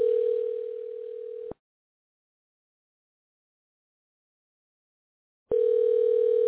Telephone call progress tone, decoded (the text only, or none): ringback tone